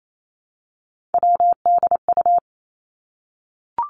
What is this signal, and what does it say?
Morse code, heard as a keyed tone, about 28 wpm, 700 Hz. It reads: WDU
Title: Three Challenging Letter-Number Combinations 28wpm